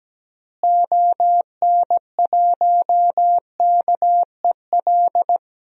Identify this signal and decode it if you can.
Morse code, heard as a keyed tone, about 17 wpm, 700 Hz. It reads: ON1KEL
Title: DXCC-100 Call Signs - 6 Characters - Repeated 3x - 17wpm